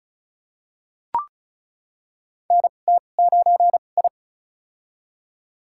Morse code, heard as a keyed tone, about 34 wpm, spaced as for 25 wpm, 700 Hz. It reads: NT9I
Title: USA Call Signs - 2x Character Spacing - 35wpm